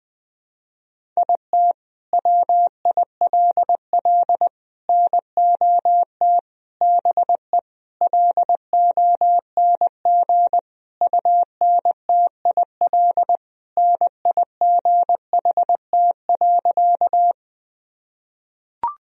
Morse code, heard as a keyed tone, about 20 wpm, 700 Hz. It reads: IT WILL NOT BE LONG UNTIL NIGHT.
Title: Sentences from Top 1000 Words - Review Part 1 - 20wpm